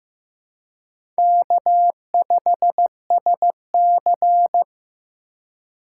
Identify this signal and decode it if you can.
Morse code, heard as a keyed tone, about 15 wpm, 700 Hz. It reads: K5SC
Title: USA Call Signs - 4 Characters - 3x Character Spacing - 15wpm